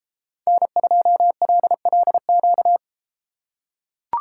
Morse code, heard as a keyed tone, about 32 wpm, 700 Hz. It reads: N2LLQ